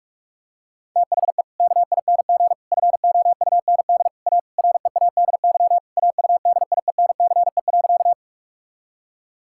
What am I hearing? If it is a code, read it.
Morse code, heard as a keyed tone, about 46 wpm, 700 Hz. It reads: THE KING FOUND A READY AUDIENCE.
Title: Sentences from Top 1000 Words - Review Part 5 - 45wpm